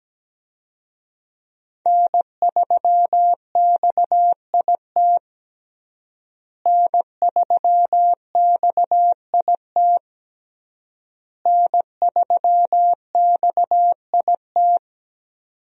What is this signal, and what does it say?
Morse code, heard as a keyed tone, about 17 wpm, 700 Hz. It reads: N3XIT N3XIT N3XIT
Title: USA Call Signs - 5 Characters - Repeated 3x - 17wpm